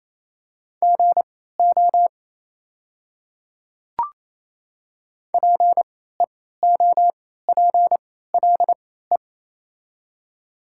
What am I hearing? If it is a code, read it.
Morse code, heard as a keyed tone, about 28 wpm, spaced as for 16 wpm, 700 Hz. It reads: GO PEOPLE
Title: 3x Character Spacing - Top 100 Words - 28wpm